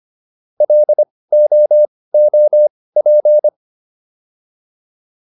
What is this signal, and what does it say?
Morse code, heard as a keyed tone, about 25 wpm, spaced as for 18 wpm, 600 Hz. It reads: LOOP